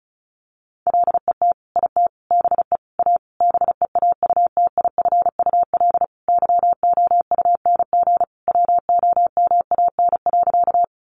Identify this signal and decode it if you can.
Morse code, heard as a keyed tone, about 34 wpm, 700 Hz. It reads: LET IT BE A BEAUTIFUL YOUNG WOMAN.
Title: Sentences from Top 1100 Words 35wpm